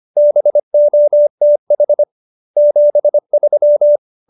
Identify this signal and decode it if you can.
Morse code, heard as a keyed tone, about 25 wpm, 590 Hz. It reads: BOTH 73